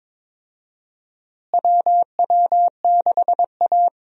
Morse code, heard as a keyed tone, about 22 wpm, 700 Hz. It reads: WW6A
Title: USA Call Signs - 4 Characters - Repeated 2x - 22wpm